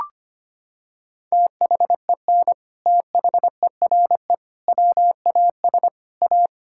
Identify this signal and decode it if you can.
Morse code, heard as a keyed tone, about 25 wpm, 700 Hz. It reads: THEN THERE WAS A
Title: Sentences from Top 1600 Words 25wpm